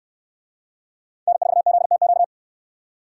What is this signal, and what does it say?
Morse code, heard as a keyed tone, about 62 wpm, 700 Hz. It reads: N4ZTQ